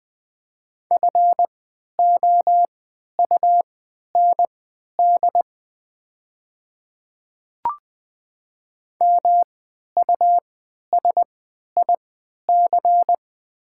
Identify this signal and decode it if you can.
Morse code, heard as a keyed tone, about 21 wpm, spaced as for 12 wpm, 700 Hz. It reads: FOUND MUSIC
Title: 3x Character Spacing - Top 300 Words - 20wpm